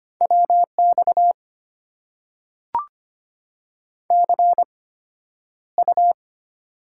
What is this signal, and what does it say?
Morse code, heard as a keyed tone, about 25 wpm, spaced as for 11 wpm, 700 Hz. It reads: WX CU
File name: Top 90 QSO Elements - 8x Character Spacing - 25wpm